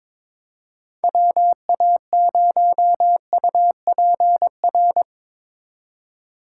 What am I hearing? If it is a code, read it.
Morse code, heard as a keyed tone, about 24 wpm, 700 Hz. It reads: WA0UPR
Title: Speed Racing - USA Call Signs 22wpm